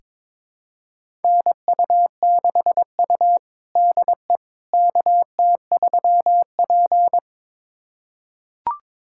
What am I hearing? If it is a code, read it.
Morse code, heard as a keyed tone, about 22 wpm, 700 Hz. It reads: NU6U DE KT3P
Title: USA Callsign DE Callsign - 4 Characters - 22wpm